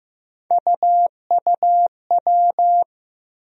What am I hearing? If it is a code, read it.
Morse code, heard as a keyed tone, about 15 wpm, 700 Hz. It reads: UUW
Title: USA Call Sign Postfixes - 3 Letters 15wpm